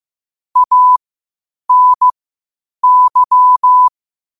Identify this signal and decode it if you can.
Morse code, heard as a keyed tone, about 15 wpm, spaced as for 8 wpm, 1000 Hz. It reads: ANY
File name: qotdfiles / 8wpm